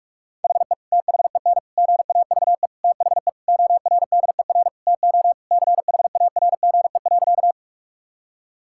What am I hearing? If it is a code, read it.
Morse code, heard as a keyed tone, about 41 wpm, 700 Hz. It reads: HE THEN GAVE THE ORDER TO CHARGE.
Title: Sentences from Top 1000 Words - Review Part 6 - 45wpm